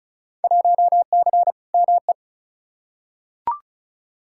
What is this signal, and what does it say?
Morse code, heard as a keyed tone, about 36 wpm, 700 Hz. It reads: 1C ME
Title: Field Day Exchanges - 35wpm